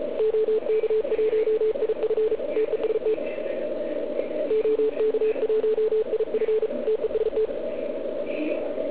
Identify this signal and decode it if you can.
Morse code, heard as a keyed tone, about 34 wpm, 440 Hz. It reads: OK1IF TEST OK1IF TEST